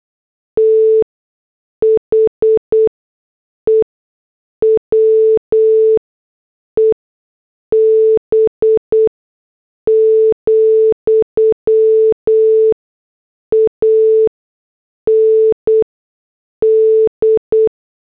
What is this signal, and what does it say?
Morse code, heard as a keyed tone, about 8 wpm, spaced as for 6 wpm, 440 Hz. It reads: THE WEB, AND
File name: morse / headlines